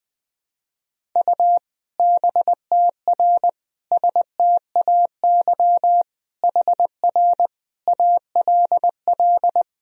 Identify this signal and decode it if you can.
Morse code, heard as a keyed tone, about 20 wpm, 700 Hz. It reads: U BTR STAY HR ALL
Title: Sentences from Top 500 Words and Abbreviations 20wpm